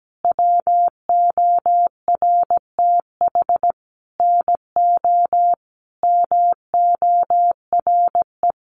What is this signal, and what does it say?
Morse code, heard as a keyed tone, about 17 wpm, 700 Hz. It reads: WORTH NO MORE
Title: Sentences from Top 800 Words 17wpm